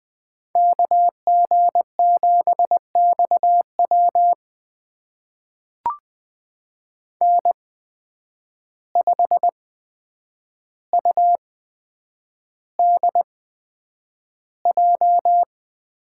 Morse code, heard as a keyed tone, about 20 wpm, spaced as for 9 wpm, 700 Hz. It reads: KG7XW N5UDJ
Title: USA Call Signs - 5 Characters - 8x Character Spacing - 20wpm